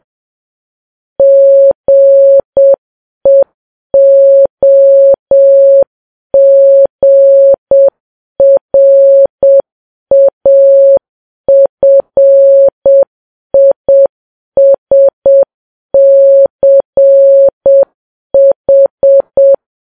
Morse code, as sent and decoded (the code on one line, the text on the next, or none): --. --- --. .-. .- ..-. .. ... -.-. ....
GEOGRAFISCH